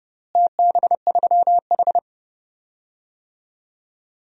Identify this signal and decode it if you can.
Morse code, heard as a keyed tone, about 31 wpm, 700 Hz. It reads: TB3H
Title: DXCC-100 Call Signs - 4 Characters - Repeated 3x - 30wpm